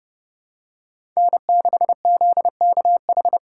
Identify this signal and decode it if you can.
Morse code, heard as a keyed tone, about 30 wpm, 700 Hz. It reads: N6ZKH
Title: USA Call Signs - 5 Characters - 4x Character Spacing - 30wpm